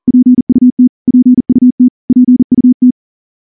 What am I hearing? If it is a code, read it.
Morse code, heard as a keyed tone, about 38 wpm, 250 Hz. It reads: PUT PUT PUT
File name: morse-training-40WPM